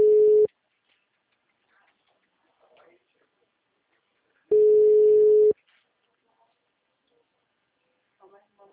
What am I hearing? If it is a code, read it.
Ringback tone (Europe, 425 Hz), with speech.
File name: out- -2033- - - .61799